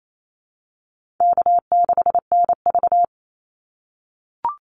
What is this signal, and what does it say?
Morse code, heard as a keyed tone, about 28 wpm, 700 Hz. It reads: K6NV
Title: Call Signs - 28wpm